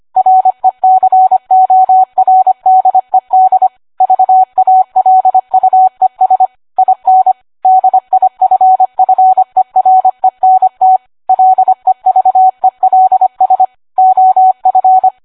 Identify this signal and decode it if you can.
Morse code, heard as a keyed tone, about 25 wpm, 750 Hz. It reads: RECORDED VALUES IN DIFFERENT LEVELS OF